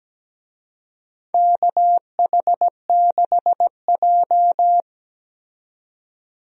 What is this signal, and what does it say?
Morse code, heard as a keyed tone, about 17 wpm, 700 Hz. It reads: KH6J
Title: DXCC-100 Call Signs - 4 Characters - 17wpm